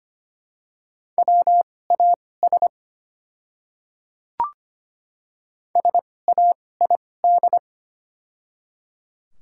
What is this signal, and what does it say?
Morse code, heard as a keyed tone, about 25 wpm, spaced as for 18 wpm, 700 Hz. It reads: WAS SAID
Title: 2x Character Spacing - Top 100 Words - 25wpm